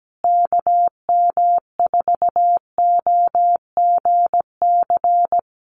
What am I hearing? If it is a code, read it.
Morse code, heard as a keyed tone, about 17 wpm, 700 Hz. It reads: KM4OGC
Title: USA Call Signs Repeated 3x 17wpm